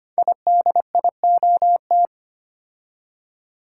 Morse code, heard as a keyed tone, about 25 wpm, 700 Hz. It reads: IDIOT
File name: 5 Letter Words - First 3 Letters Given - 25wpm